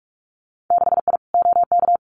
Morse code, heard as a keyed tone, about 46 wpm, 700 Hz. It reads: 6I OK
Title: Winter Field Day Exchanges Encore 45wpm